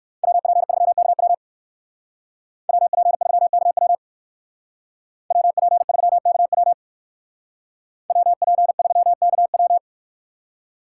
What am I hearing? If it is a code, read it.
Morse code, heard as a keyed tone, about 59 wpm, 690 Hz. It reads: WP3KW WP3KW WP3KW WP3KW